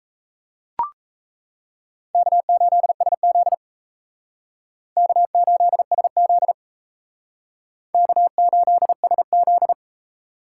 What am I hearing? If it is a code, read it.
Morse code, heard as a keyed tone, about 35 wpm, 700 Hz. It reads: K8SZ K8SZ K8SZ